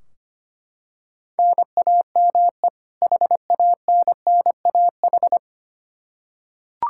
Morse code, heard as a keyed tone, about 25 wpm, 700 Hz. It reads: NAME HANNAH
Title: Practice QSO - Exchange 1 - Beginning - 25wpm